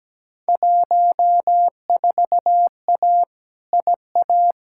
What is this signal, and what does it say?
Morse code, heard as a keyed tone, about 17 wpm, 700 Hz. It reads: 14A IA
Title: Field Day Exchanges Encore - 17wpm